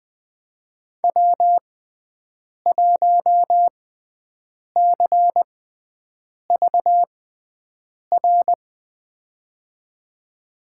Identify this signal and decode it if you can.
Morse code, heard as a keyed tone, about 20 wpm, spaced as for 7 wpm, 700 Hz. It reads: W1CVR
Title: USA Call Signs - 5 Characters - 6x Character Spacing - 20wpm